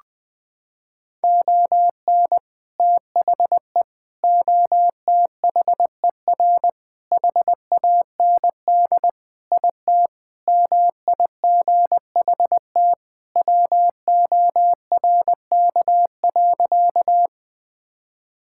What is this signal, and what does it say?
Morse code, heard as a keyed tone, about 20 wpm, 700 Hz. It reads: ON THE OTHER HAND IT MIGHT WORK.
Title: Sentences with 3-Word Phrases - 20wpm